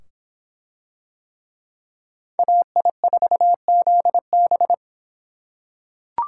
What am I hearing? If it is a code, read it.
Morse code, heard as a keyed tone, about 25 wpm, 700 Hz. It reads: AI4ZB